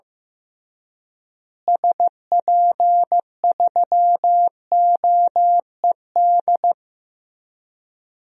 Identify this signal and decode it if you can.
Morse code, heard as a keyed tone, about 15 wpm, 700 Hz. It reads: SP3OED
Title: DXCC-100 Call Signs - 6 Characters - Repeated 3x - 15wpm